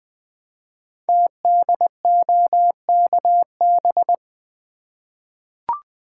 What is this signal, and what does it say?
Morse code, heard as a keyed tone, about 20 wpm, 700 Hz. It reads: TDOKB
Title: Code Groups - 5 Letters - 20wpm